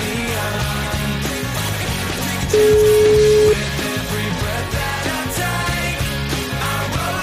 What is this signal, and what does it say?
Ringback tone (Europe, 425 Hz), with noise.